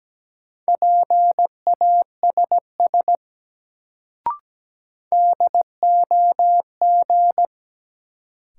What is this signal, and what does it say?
Morse code, heard as a keyed tone, about 17 wpm, 700 Hz. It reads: PASS DOG
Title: Sentences from Top 400 Words - New Word Drill - 17wpm